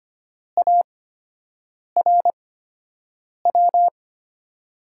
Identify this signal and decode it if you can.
Morse code, heard as a keyed tone, about 25 wpm, spaced as for 7 wpm, 700 Hz. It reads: ARW
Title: USA Call Signs - 5 Characters - 8x Character Spacing - 25wpm